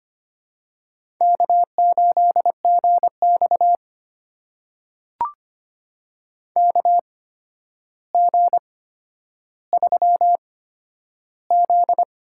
Morse code, heard as a keyed tone, about 25 wpm, spaced as for 11 wpm, 700 Hz. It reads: K8GX KG3Z
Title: USA Call Signs - 8x Character Spacing - 25wpm